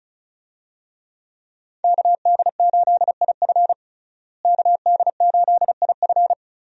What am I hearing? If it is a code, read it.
Morse code, heard as a keyed tone, about 36 wpm, 700 Hz. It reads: KD8IF KD8IF
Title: USA Call Signs - 5 Characters - Repeated 2x - 35wpm